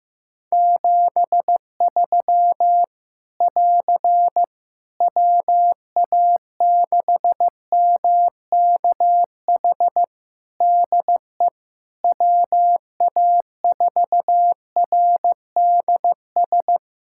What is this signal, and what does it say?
Morse code, heard as a keyed tone, about 16 wpm, 700 Hz. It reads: 73 <AR> WA6MKH DE WA4RDS
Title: Practice QSO - Exchange 4 - Ending - 15wpm